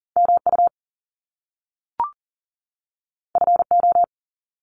Morse code, heard as a keyed tone, about 40 wpm, 700 Hz. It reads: MU FO
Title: Top Two Letter Combinations in Words 40wpm